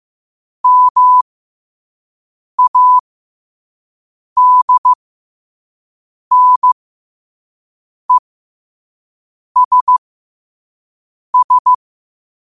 Morse code, heard as a keyed tone, about 15 wpm, spaced as for 5 wpm, 1000 Hz. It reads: MADNESS
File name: qotdfiles / 5wpm